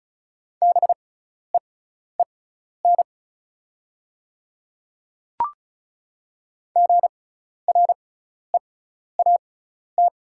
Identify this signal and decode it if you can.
Morse code, heard as a keyed tone, about 36 wpm, spaced as for 12 wpm, 690 Hz. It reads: BEEN GREAT
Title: 6x Character Spacing - Top 200 Words - 35wpm